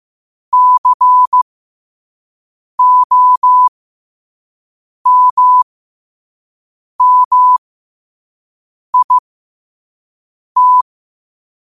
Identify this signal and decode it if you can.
Morse code, heard as a keyed tone, about 15 wpm, spaced as for 5 wpm, 1000 Hz. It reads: COMMIT
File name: qotdfiles / 5wpm